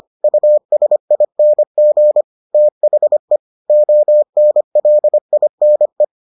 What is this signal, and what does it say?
Morse code, heard as a keyed tone, about 25 wpm, 600 Hz. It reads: USING THE ONLINE